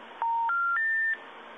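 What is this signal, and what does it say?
Special information tone, with noise.
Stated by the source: Telephone call progress tone recorded in Virginia, modern North American telephone system.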